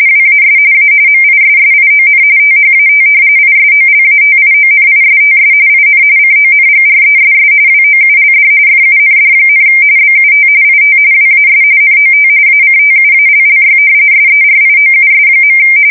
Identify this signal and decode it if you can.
RTTY, about 45.45 baud. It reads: WAS A MISSION THAT ISOLATIONISTS IN CONGRESS WOULD FUND, BUT IT WAS NOT ACTUALLY SERIOUS.
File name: fsk170 / usnavy4